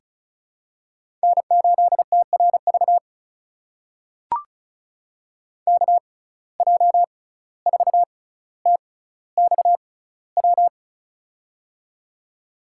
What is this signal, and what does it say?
Morse code, heard as a keyed tone, about 36 wpm, spaced as for 17 wpm, 700 Hz. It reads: N8TRV KJ4TXW
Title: USA Call Signs - 6x Character Spacing - 35wpm